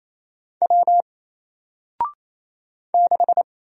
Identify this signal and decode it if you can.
Morse code, heard as a keyed tone, about 28 wpm, 700 Hz. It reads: W 6
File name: Single-Letter-Number-28wpm